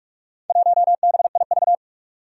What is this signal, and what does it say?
Morse code, heard as a keyed tone, about 45 wpm, 700 Hz. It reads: 1BIV
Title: Four Challenging Letter-Number Combinations 45wpm